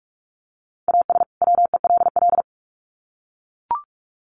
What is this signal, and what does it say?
Morse code, heard as a keyed tone, about 44 wpm, 700 Hz. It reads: AS WELL
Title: Sets of 2 Words 45wpm